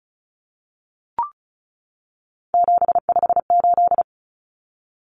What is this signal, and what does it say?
Morse code, heard as a keyed tone, about 34 wpm, 700 Hz. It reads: 758
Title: Common Cut Numbers - 3-Digit 35wpm